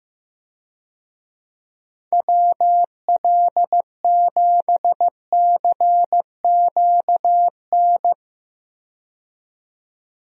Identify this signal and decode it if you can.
Morse code, heard as a keyed tone, about 16 wpm, 700 Hz. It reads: WL7CQN
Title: USA Call Signs - 6 Characters - 15wpm